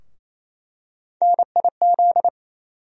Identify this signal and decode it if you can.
Morse code, heard as a keyed tone, about 28 wpm, 700 Hz. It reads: NIZ
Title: USA Call Sign Postfixes - 3 Letters 28wpm